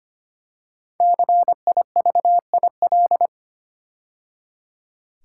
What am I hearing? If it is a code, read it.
Morse code, heard as a keyed tone, about 25 wpm, 700 Hz. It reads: CIVIL
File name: 5 Letter Words - First 3 Letters Given - 25wpm